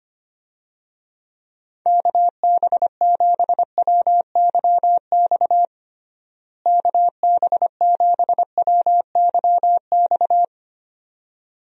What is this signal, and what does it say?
Morse code, heard as a keyed tone, about 25 wpm, 700 Hz. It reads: KB7WYX KB7WYX
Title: USA Call Signs - 6 Characters - Repeated 2x - 25wpm